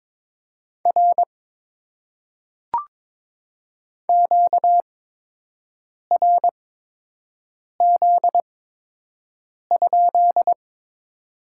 Morse code, heard as a keyed tone, about 22 wpm, spaced as for 6 wpm, 700 Hz. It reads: R QRZ?
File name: Top 90 QSO Elements - 8x Character Spacing - 22wpm